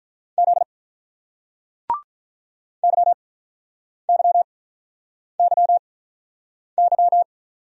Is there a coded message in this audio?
Yes: Morse code from 0.4 to 7.2 s, heard as a keyed tone, about 46 wpm, spaced as for 9 wpm, 700 Hz. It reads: Z YYYY